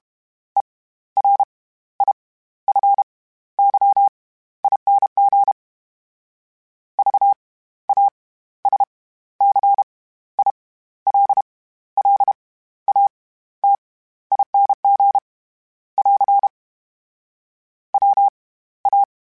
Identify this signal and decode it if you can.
Morse code, heard as a keyed tone, about 32 wpm, 800 Hz. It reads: E R I F Y ING V A S C I L L A T ING <AR> W A
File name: Special INGs / Special INGs-mp3s / SlowToFaster ing @30